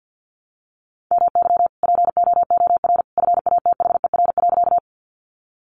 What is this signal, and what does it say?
Morse code, heard as a keyed tone, about 50 wpm, 700 Hz. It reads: MY POOR FATHER.
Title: Sentences from Top 1000 Words 50wpm